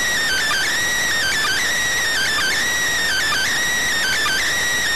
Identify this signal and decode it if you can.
Emergency Alert System (SAME) tones, heard as harsh buzzing bursts. It is heard from 0.2 to 4.4 s.